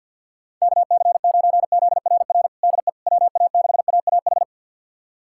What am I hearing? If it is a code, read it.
Morse code, heard as a keyed tone, about 48 wpm, 700 Hz. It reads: KC9ZRR DE WA6AAH
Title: USA Callsign DE Callsign 50wpm